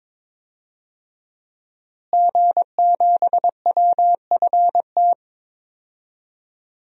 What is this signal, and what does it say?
Morse code, heard as a keyed tone, about 22 wpm, 700 Hz. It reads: G7WFT
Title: DXCC-100 Call Signs - 5 Characters - Repeated 2x - 22wpm